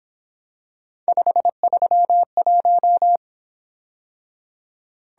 Morse code, heard as a keyed tone, about 27 wpm, 700 Hz. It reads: 531
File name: Speed-Racing - Three Letter-Number Combinations - 22wpm